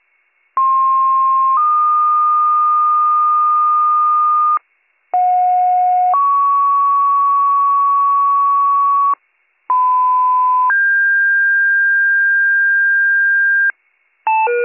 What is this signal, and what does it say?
Two-tone page, heard as a one-second tone, then a three-second tone, about 1060 then 1220 Hz, about 730 then 1120 Hz, about 1010 then 1600 Hz. It is heard from 0.6 to 13.7 s.